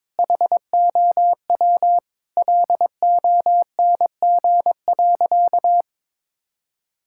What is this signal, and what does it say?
Morse code, heard as a keyed tone, about 22 wpm, 700 Hz. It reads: HOW LONG.
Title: Sentences from Top 1000 Words - Review Part 6 - 22wpm